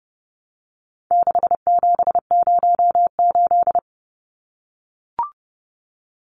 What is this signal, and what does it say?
Morse code, heard as a keyed tone, about 30 wpm, 700 Hz. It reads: 6708
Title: Numbers - 4-Digit 30wpm